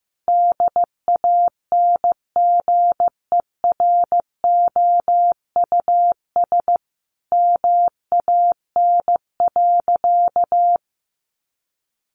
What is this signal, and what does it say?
Morse code, heard as a keyed tone, about 15 wpm, 700 Hz. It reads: DANGEROUS MAN.
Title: Sentences from Top 1200 Words 15wpm